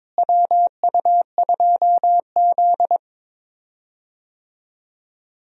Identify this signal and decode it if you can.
Morse code, heard as a keyed tone, about 22 wpm, 700 Hz. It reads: WU2Z
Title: USA Call Signs - 4 Characters - Repeated 3x - 22wpm